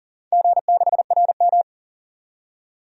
Morse code, heard as a keyed tone, about 40 wpm, 700 Hz. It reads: G6RM
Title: DXCC-100 Call Signs - 4 Characters - 40wpm